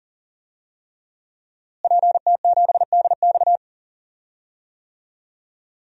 Morse code, heard as a keyed tone, about 39 wpm, 700 Hz. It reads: PT7DX
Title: DXCC-100 Call Signs - 5 Characters - Repeated 3x - 40wpm